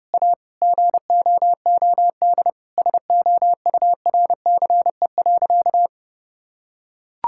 Morse code, heard as a keyed tone, about 30 wpm, 700 Hz. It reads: A GOOD SOURCE.
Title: Sentences from Top 1000 Words - Review Part 5 - 30wpm